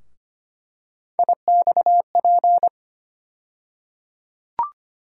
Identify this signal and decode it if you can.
Morse code, heard as a keyed tone, about 25 wpm, 700 Hz. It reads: IXP
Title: USA Call Sign Postfixes - 3 Letters 25wpm